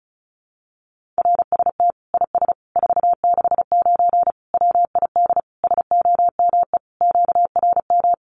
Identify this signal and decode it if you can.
Morse code, heard as a keyed tone, about 34 wpm, 700 Hz. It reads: RST IS 469 WID SOME QRM
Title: Common QSO Phrases 35wpm